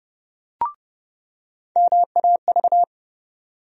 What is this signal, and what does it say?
Morse code, heard as a keyed tone, about 30 wpm, 700 Hz. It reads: MAV
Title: USA Call Sign Postfixes - 3 Letters 30wpm